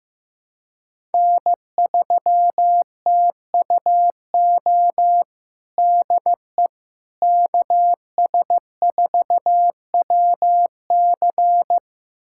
Morse code, heard as a keyed tone, about 16 wpm, 700 Hz. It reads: N3TUO DE KS4WC
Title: USA Callsign DE Callsign - 5 Characters - 15wpm